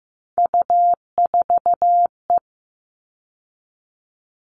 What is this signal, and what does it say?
Morse code, heard as a keyed tone, about 15 wpm, 700 Hz. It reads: U4E
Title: All Cut Numbers - 3-Digit 15wpm